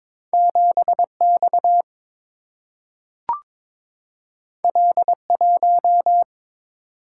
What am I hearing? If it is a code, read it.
Morse code, heard as a keyed tone, about 22 wpm, 700 Hz. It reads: 7X L1
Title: Single-Letter Single-Number 22wpm